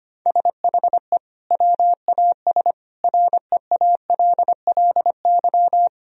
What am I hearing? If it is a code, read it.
Morse code, heard as a keyed tone, about 25 wpm, 700 Hz. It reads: SHE WAS REALLY
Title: Sentences from Top 1000 Words - Review Part 4 - 25wpm